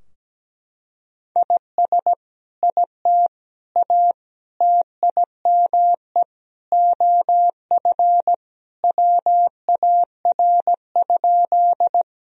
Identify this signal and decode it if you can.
Morse code, heard as a keyed tone, about 17 wpm, 700 Hz. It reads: IS IT A TIME OF WAR?
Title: Sentences from Top 1000 Words - Review Part 2 - 17wpm